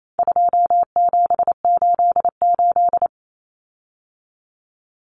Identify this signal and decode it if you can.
Morse code, heard as a keyed tone, about 28 wpm, 700 Hz. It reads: 2788
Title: Common Cut Numbers - 4-Digit 28wpm